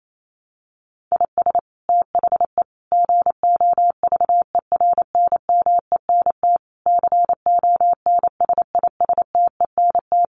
Morse code, heard as a keyed tone, about 28 wpm, 700 Hz. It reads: IS THE GOVERNMENT CONSISTENT
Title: Sentences from Top 1500 Words 28wpm